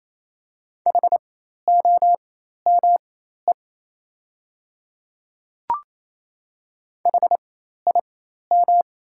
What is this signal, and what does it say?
Morse code, heard as a keyed tone, about 27 wpm, spaced as for 13 wpm, 700 Hz. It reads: HOME HIM
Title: 4x Character Spacing - Top 300 Words - 28wpm